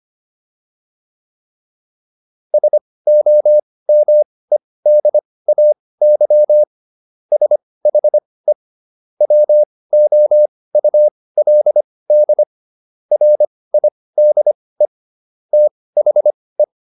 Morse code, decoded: SOMEDAY SHE WOULD RIDE THE

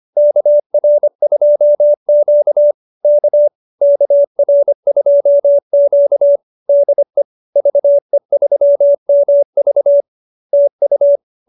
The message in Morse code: KR2Q K KR2Q DE VE3MV TU

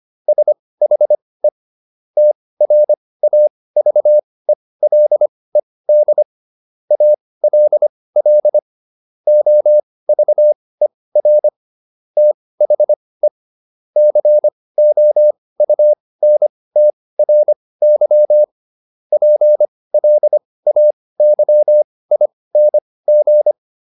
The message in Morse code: SHE TRAVELED ALL OVER THE COUNTRY PLAYING

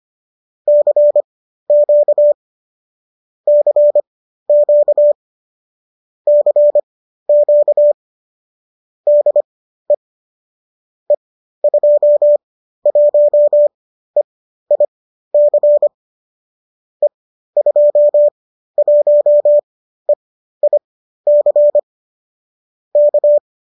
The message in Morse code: CQ CQ CQ DE E21EIC E21EIC K

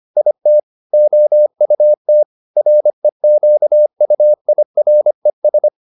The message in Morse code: IT OUT REQUIRES

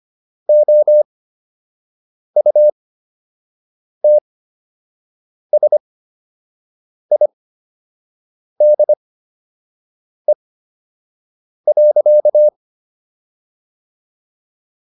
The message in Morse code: OUTSIDE.

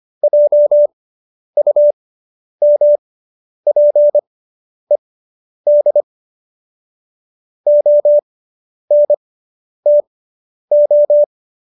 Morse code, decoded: JUMPED ONTO

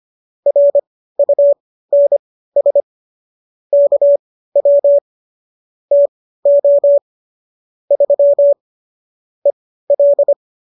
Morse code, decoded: RUNS KW TO 3 EL

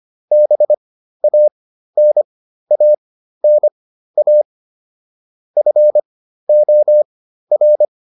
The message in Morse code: BANANA FOR

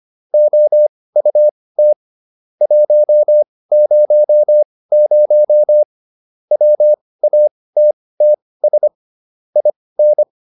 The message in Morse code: OUT 100 WATTS IN

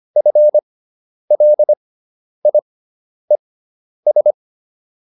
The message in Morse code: FLIES

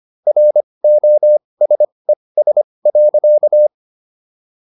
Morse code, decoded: ROSES.

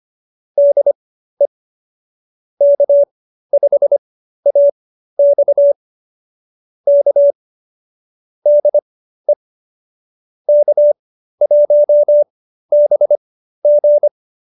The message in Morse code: DE K5AX K DE K1BG

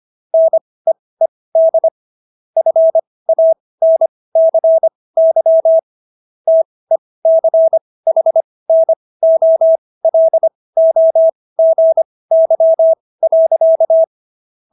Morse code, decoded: NEED FANCY TECHNOLOGY.

